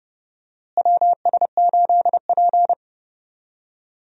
Morse code, heard as a keyed tone, about 30 wpm, 700 Hz. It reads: WS8P
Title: USA Call Signs - 4 Characters - Repeated 2x - 30wpm